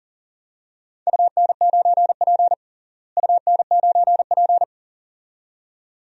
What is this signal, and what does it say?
Morse code, heard as a keyed tone, about 40 wpm, 700 Hz. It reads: UN9P UN9P
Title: DXCC-100 Call Signs - 4 Characters - Repeated 2x - 40wpm